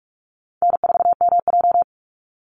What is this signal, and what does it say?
Morse code, heard as a keyed tone, about 46 wpm, 700 Hz. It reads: N4MJ
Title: NA SOTA Chaser Callsigns 45wpm